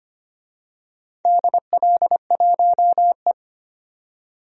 Morse code, heard as a keyed tone, about 25 wpm, 700 Hz. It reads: DL1E
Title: DXCC-100 Call Signs - 4 Characters - 25wpm